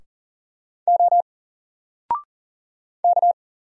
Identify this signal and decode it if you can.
Morse code, heard as a keyed tone, about 40 wpm, 700 Hz. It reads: O K
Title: Single Letter-Number Alt 40wpm